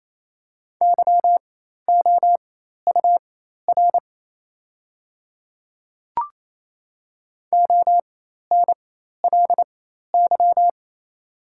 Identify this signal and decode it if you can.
Morse code, heard as a keyed tone, about 28 wpm, spaced as for 13 wpm, 700 Hz. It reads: YOUR ONLY
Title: 4x Character Spacing - Top 200 Words - 28wpm